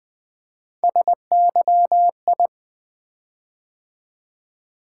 Morse code, heard as a keyed tone, about 20 wpm, 700 Hz. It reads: SYI
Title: Three Letter-Number Combinations - 20wpm